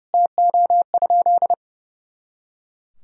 Morse code, heard as a keyed tone, about 30 wpm, 700 Hz. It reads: TO?
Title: Sentences from Top 100 Words and Abbreviations 30wpm